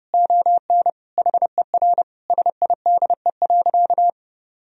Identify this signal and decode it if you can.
Morse code, heard as a keyed tone, about 30 wpm, 700 Hz. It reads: ON HER SIDE.
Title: Sentences from Top 1000 Words - Review Part 1 - 30wpm